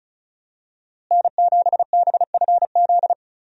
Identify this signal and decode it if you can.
Morse code, heard as a keyed tone, about 36 wpm, 700 Hz. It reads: N7BFZ
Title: USA Call Signs - 5 Characters - Repeated 2x - 35wpm